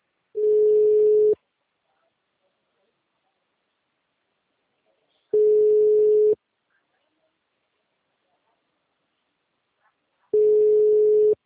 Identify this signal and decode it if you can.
Ringback tone (Europe, 425 Hz).